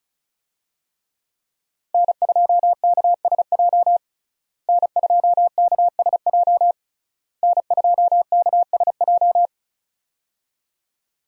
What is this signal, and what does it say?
Morse code, heard as a keyed tone, about 36 wpm, 700 Hz. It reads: N2KSJ N2KSJ N2KSJ
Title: USA Call Signs - 5 Characters - Repeated 3x - 35wpm